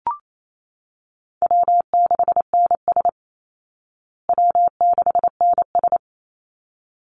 Morse code, heard as a keyed tone, about 28 wpm, 700 Hz. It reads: W6NS W6NS
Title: Call Signs Repeated 3x - 28wpm